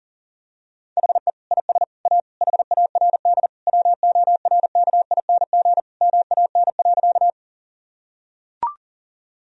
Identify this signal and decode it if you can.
Morse code, heard as a keyed tone, about 37 wpm, 700 Hz. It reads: HE IS A HARD WORKING MAN.
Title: Sentences from Top 1000 Words - Review Part 4 - 40wpm